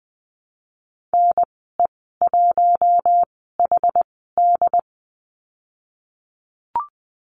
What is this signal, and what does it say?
Morse code, heard as a keyed tone, about 19 wpm, spaced as for 14 wpm, 700 Hz. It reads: NE1HD
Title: 128 Lesson 21g - Number 6 - Call Signs 20wpm